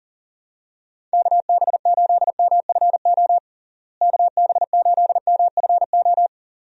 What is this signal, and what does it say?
Morse code, heard as a keyed tone, about 40 wpm, 700 Hz. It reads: KB8MFO KB8MFO
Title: USA Call Signs - 6 Characters - Repeated 2x - 40wpm